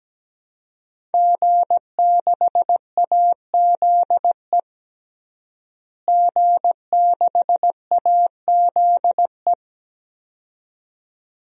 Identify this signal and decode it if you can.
Morse code, heard as a keyed tone, about 17 wpm, 700 Hz. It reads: G6AZE G6AZE
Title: DXCC-100 Call Signs - 5 Characters - Repeated 2x - 17wpm